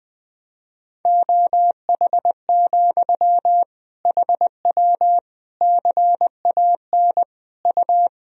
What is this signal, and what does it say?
Morse code, heard as a keyed tone, about 20 wpm, 700 Hz. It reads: OH, HW CAN U
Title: Sentences from Top 500 Words and Abbreviations 20wpm